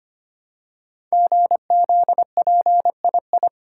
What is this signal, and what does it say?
Morse code, heard as a keyed tone, about 25 wpm, 700 Hz. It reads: GZPII
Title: Code Groups - 5 Letter:Numbers - 25wpm